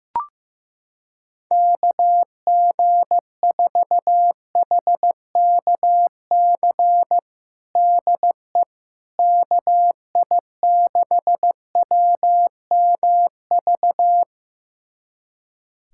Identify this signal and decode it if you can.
Morse code, heard as a keyed tone, about 15 wpm, 700 Hz. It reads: KG4HKC DE KI6WMV
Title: USA Callsign DE Callsign 15wpm